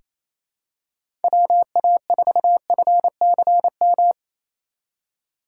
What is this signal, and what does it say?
Morse code, heard as a keyed tone, about 28 wpm, 700 Hz. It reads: WA4FCM
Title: USA Call Signs - 6 Characters - 4x Character Spacing - 28wpm